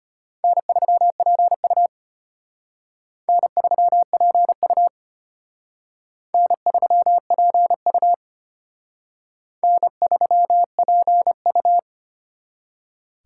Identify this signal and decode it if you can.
Morse code, heard as a keyed tone, about 33 wpm, 700 Hz. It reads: N3PU N3PU N3PU N3PU